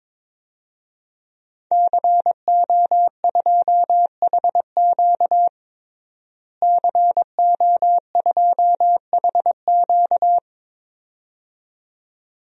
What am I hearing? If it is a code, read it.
Morse code, heard as a keyed tone, about 23 wpm, 700 Hz. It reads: CO2HQ CO2HQ
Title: DXCC-100 Call Signs - 5 Characters - Repeated 2x - 22wpm